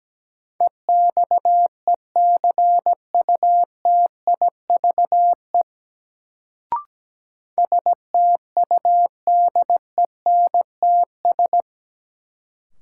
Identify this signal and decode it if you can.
Morse code, heard as a keyed tone, about 17 wpm, 700 Hz. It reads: EXECUTIVE STUDENTS
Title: Sets of 5 Words Encore - New Word Drill - 17wpm